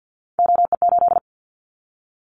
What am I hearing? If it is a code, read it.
Morse code, heard as a keyed tone, about 50 wpm, 700 Hz. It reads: OE8
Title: Callsign Prefixes for Top 50 DXCC - 50wpm